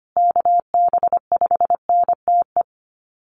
Morse code, heard as a keyed tone, about 25 wpm, 700 Hz. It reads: KB5NTE
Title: USA Call Signs Repeated 3x 25wpm